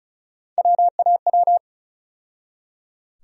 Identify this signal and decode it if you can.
Morse code, heard as a keyed tone, about 34 wpm, 700 Hz. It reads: WAW
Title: USA Call Sign Postfixes - 3 Letters 35wpm